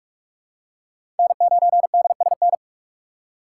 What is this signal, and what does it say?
Morse code, heard as a keyed tone, about 45 wpm, 700 Hz. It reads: N9DSN
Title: USA Call Signs - 5 Characters - 2x Character Spacing - 45wpm